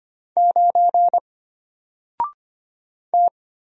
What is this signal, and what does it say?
Morse code, heard as a keyed tone, about 25 wpm, 700 Hz. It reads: 9 T
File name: Single-Letter-Number-25wpm